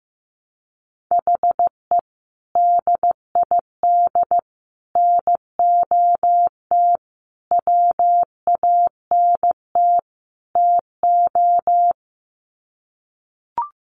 Morse code, heard as a keyed tone, about 15 wpm, 700 Hz. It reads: HE DID NOT WANT TO
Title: Sets of 5 Words 15wpm